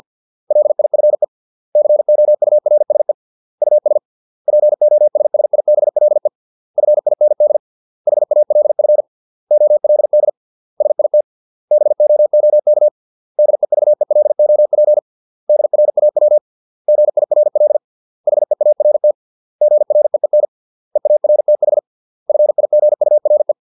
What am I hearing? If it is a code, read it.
Morse code, heard as a keyed tone, about 50 wpm, 600 Hz. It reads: LIFE COURSE US POSSIBLE FIND HALF OLD SIT BOOK DEVELOP DRAW GIRL HEART GREEN EARTH FIGURE